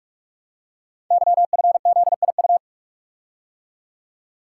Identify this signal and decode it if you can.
Morse code, heard as a keyed tone, about 44 wpm, 690 Hz. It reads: YFZIU
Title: Code Groups - 5 Letters - 45wpm